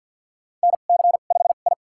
Morse code, heard as a keyed tone, about 46 wpm, spaced as for 34 wpm, 700 Hz. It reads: NC5I
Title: USA Call Signs - 4 Characters - 2x Character Spacing - 50wpm